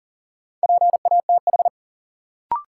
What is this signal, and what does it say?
Morse code, heard as a keyed tone, about 40 wpm, 700 Hz. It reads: PATH
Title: Sentences from Top 1400 Words - New Word Drill - 40wpm